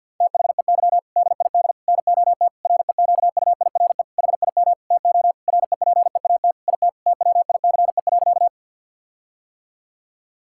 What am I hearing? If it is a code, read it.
Morse code, heard as a keyed tone, about 50 wpm, 700 Hz. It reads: THEY DID NOT REQUIRE HIM TO REPEAT IT TWICE.
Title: Sentences from Top 1600 Words 50wpm